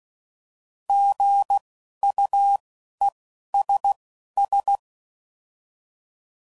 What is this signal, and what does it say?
Morse code, heard as a keyed tone, about 16 wpm, spaced as for 11 wpm, 800 Hz. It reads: GUESS
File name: Prefixes / By First Letter